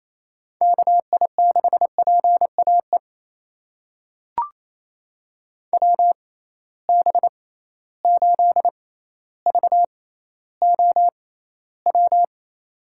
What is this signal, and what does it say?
Morse code, heard as a keyed tone, about 28 wpm, spaced as for 14 wpm, 700 Hz. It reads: KI6PAE WB8VOW
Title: USA Call Signs - 6 Characters - 6x Character Spacing - 28wpm